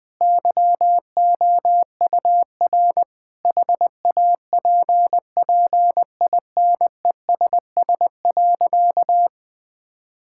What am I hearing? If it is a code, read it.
Morse code, heard as a keyed tone, about 20 wpm, 700 Hz. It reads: YOUR HAPPINESS.
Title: Sentences with 6-Word Phrases - 20wpm